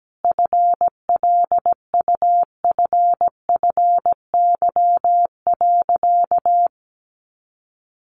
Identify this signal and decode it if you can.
Morse code, heard as a keyed tone, about 17 wpm, 700 Hz. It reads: FLUFFY.
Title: Easy Phrases 17wpm